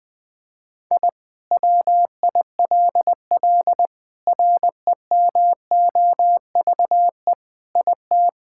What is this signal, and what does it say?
Morse code, heard as a keyed tone, about 20 wpm, 700 Hz. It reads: I WILL REMOVE IT
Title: Sentences from Top 1000 Words - Review Part 5 - 20wpm